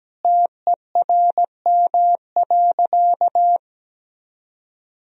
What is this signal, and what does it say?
Morse code, heard as a keyed tone, about 17 wpm, 700 Hz. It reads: TERM.
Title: Sentences from Top 1000 Words - Review Part 4 - 17wpm